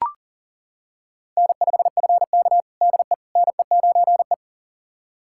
Morse code, heard as a keyed tone, about 40 wpm, 700 Hz. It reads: N5FK DE NE9E
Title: USA Callsign DE Callsign - 4 Characters - 40wpm